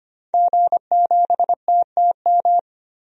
Morse code, heard as a keyed tone, about 25 wpm, 700 Hz. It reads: G7TTM
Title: DXCC-100 Call Signs - 5 Characters - 25wpm